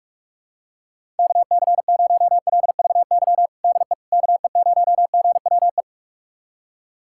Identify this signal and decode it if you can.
Morse code, heard as a keyed tone, about 46 wpm, 690 Hz. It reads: KC0LVY DE KE0GWE
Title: USA Callsign DE Callsign 45wpm